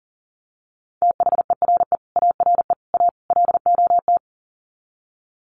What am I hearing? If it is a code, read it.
Morse code, heard as a keyed tone, about 40 wpm, 700 Hz. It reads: THERE ARE A LOT
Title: Sets of 4 Words 40wpm